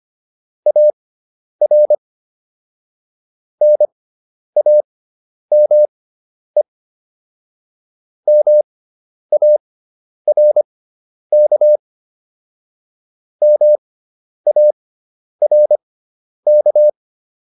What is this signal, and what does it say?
Morse code, heard as a keyed tone, about 25 wpm, spaced as for 10 wpm, 600 Hz. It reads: AR NAME MARK MARK